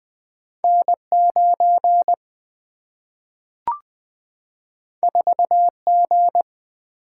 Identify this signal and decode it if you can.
Morse code, heard as a keyed tone, about 20 wpm, 700 Hz. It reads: N 9 4 G
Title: Single-Letter Single-Number 20wpm